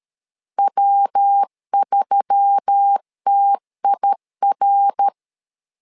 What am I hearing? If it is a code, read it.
Morse code, heard as a keyed tone, about 13 wpm, 800 Hz. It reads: W3TIR
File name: LEZIONI / 4 Q1S3 12wpm 2023